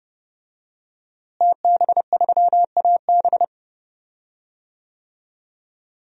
Morse code, heard as a keyed tone, about 30 wpm, 700 Hz. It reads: TB3AB
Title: DXCC-100 Call Signs - 5 Characters - Repeated 3x - 30wpm